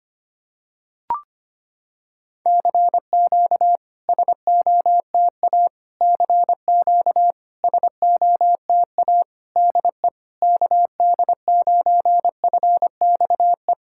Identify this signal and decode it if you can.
Morse code, heard as a keyed tone, about 25 wpm, 700 Hz. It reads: CQ SOTA CQ SOTA DE KD9FXE
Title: SOTA QSO Practice 25wpm